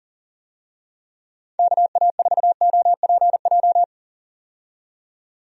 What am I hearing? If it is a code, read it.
Morse code, heard as a keyed tone, about 40 wpm, 700 Hz. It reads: KA4OPJ
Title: USA Call Signs - 6 Characters - 40wpm